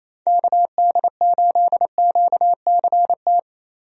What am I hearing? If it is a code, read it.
Morse code, heard as a keyed tone, about 28 wpm, 700 Hz. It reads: KD8QCT
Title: USA Call Signs - 6 Characters - 28wpm